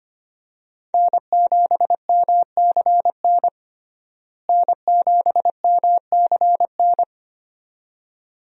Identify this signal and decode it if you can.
Morse code, heard as a keyed tone, about 25 wpm, 700 Hz. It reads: N7MCN N7MCN
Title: USA Call Signs - 5 Characters - Repeated 2x - 25wpm